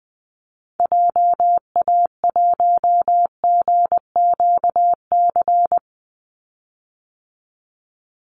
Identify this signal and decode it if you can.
Morse code, heard as a keyed tone, about 20 wpm, 700 Hz. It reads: JA1GQC
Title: Call Signs - 20wpm